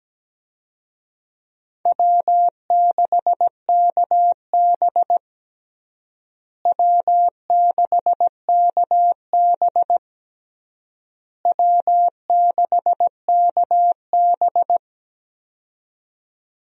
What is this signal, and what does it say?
Morse code, heard as a keyed tone, about 17 wpm, 700 Hz. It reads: W6KB W6KB W6KB
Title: USA Call Signs - 4 Characters - Repeated 3x - 17wpm